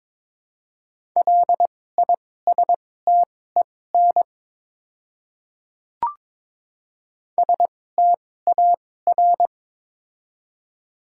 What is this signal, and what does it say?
Morse code, heard as a keyed tone, about 23 wpm, spaced as for 16 wpm, 700 Hz. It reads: LISTEN STAR
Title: 2x Character Spacing - Top 500 Words - 22wpm